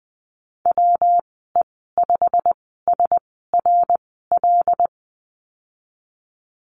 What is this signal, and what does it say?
Morse code, heard as a keyed tone, about 19 wpm, spaced as for 14 wpm, 700 Hz. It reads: WE5SRL
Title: 147 Lesson 24g - Letter y - Call Signs 20wpm